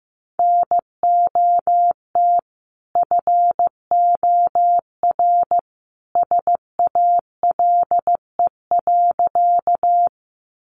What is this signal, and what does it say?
Morse code, heard as a keyed tone, about 15 wpm, 700 Hz. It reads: NOT FOR SALE.
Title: Sentences from Top 1400 Words 15wpm